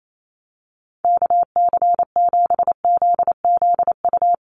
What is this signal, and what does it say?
Morse code, heard as a keyed tone, about 28 wpm, 700 Hz. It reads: KC7ZZU
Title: USA Call Signs 28wpm